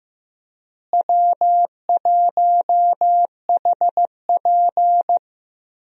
Morse code, heard as a keyed tone, about 15 wpm, 700 Hz. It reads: W1HP
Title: USA Call Signs - 4 Characters - Repeated 3x - 15wpm